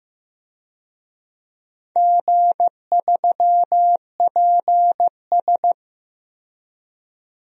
Morse code, heard as a keyed tone, about 15 wpm, 700 Hz. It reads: G3PS
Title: DXCC-100 Call Signs - 4 Characters - 15wpm